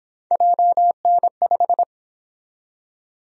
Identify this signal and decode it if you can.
Morse code, heard as a keyed tone, about 27 wpm, 700 Hz. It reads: JN5
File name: Speed-Racing - Three Letter-Number Combinations - 22wpm